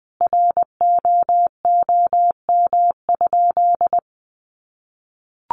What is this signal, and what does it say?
Morse code, heard as a keyed tone, about 20 wpm, 700 Hz. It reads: ROOM?
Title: Sentences from Top 1200 Words 20wpm